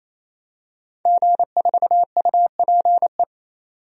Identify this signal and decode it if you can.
Morse code, heard as a keyed tone, about 28 wpm, 700 Hz. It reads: G4UPE
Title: DXCC-100 Call Signs - 5 Characters - 28wpm